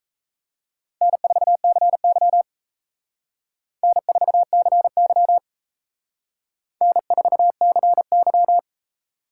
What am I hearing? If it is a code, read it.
Morse code, heard as a keyed tone, about 36 wpm, 700 Hz. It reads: N4CY N4CY N4CY